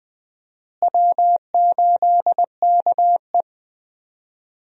Morse code, heard as a keyed tone, about 20 wpm, 700 Hz. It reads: W8KE
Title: USA Call Signs - 4 Characters - Repeated 3x - 20wpm